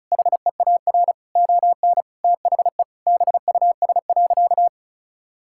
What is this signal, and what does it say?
Morse code, heard as a keyed tone, about 34 wpm, 700 Hz. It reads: HEAR ON THE BUS.
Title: Sentences from Top 1000 Words - Review Part 6 - 35wpm